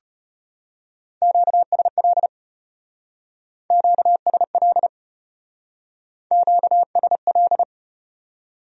Morse code, heard as a keyed tone, about 34 wpm, 700 Hz. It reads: QSL QSL QSL